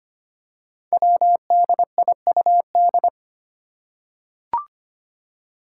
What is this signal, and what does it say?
Morse code, heard as a keyed tone, about 25 wpm, 700 Hz. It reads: WDIUD
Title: Code Groups - 5 Letter:Numbers - 25wpm